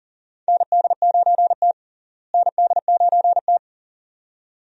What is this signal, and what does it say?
Morse code, heard as a keyed tone, about 40 wpm, 700 Hz. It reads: ND9T ND9T
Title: USA Call Signs - 4 Characters - Repeated 2x - 40wpm